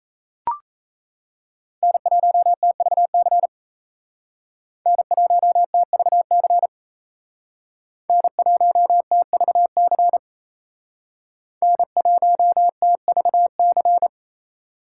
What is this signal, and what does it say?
Morse code, heard as a keyed tone, about 35 wpm, 700 Hz. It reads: N1TVC N1TVC N1TVC N1TVC